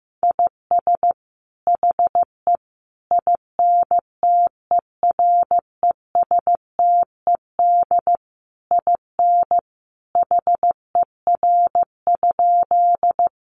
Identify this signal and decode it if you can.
Morse code, heard as a keyed tone, about 15 wpm, 700 Hz. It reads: IS HE INTERESTED IN HER?
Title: Sentences from Top 900 Words 15wpm